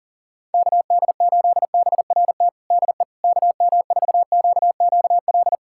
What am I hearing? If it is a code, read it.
Morse code, heard as a keyed tone, about 40 wpm, 700 Hz. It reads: KD8BRT DE KM4QQL
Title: USA Callsign DE Callsign 40wpm